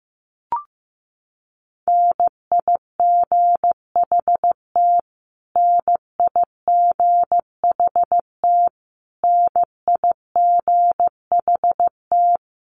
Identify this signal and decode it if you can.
Morse code, heard as a keyed tone, about 15 wpm, 700 Hz. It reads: NIGHT NIGHT NIGHT
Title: Top 300 Words Repeated 15wpm